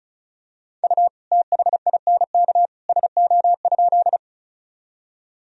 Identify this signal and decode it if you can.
Morse code, heard as a keyed tone, about 34 wpm, 690 Hz. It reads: U THINK SO?
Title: Sentences from Top 200 Words and Abbreviations 35wpm